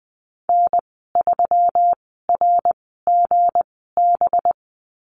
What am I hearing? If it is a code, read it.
Morse code, heard as a keyed tone, about 20 wpm, spaced as for 15 wpm, 700 Hz. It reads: N3RGB
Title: 219 Lesson 35g - Number 0 - Call Signs 20wpm